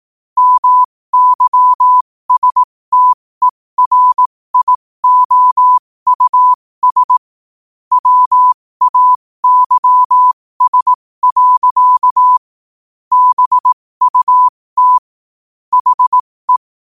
Morse code, heard as a keyed tone, about 18 wpm, spaced as for 15 wpm, 1000 Hz. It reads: MYSTERIOUS WAYS. BUT HE
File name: qotdfiles / 15wpm